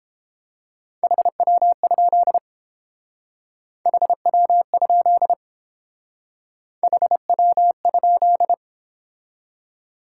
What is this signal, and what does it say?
Morse code, heard as a keyed tone, about 29 wpm, 700 Hz. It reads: HW? HW? HW?